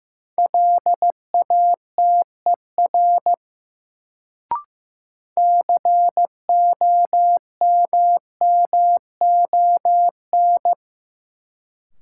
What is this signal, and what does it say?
Morse code, heard as a keyed tone, about 15 wpm, 700 Hz. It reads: LATER COMMON
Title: Sentences from Top 700 Words - New Word Drill - 15wpm